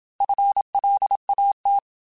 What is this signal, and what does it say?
Morse code, heard as a keyed tone, about 27 wpm, 800 Hz. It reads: FLAT
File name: FL / FL mp3s / Fast FL @25